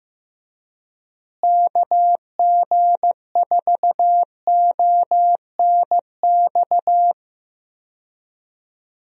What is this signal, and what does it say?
Morse code, heard as a keyed tone, about 15 wpm, 700 Hz. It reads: KG4ONX
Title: USA Call Signs - 6 Characters - Repeated 2x - 15wpm